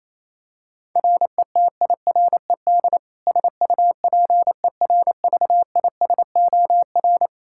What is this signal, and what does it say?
Morse code, heard as a keyed tone, about 28 wpm, 700 Hz. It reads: RETIRED SUPERVISOR
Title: Practice QSO - Exchange 3 - More Info - 28wpm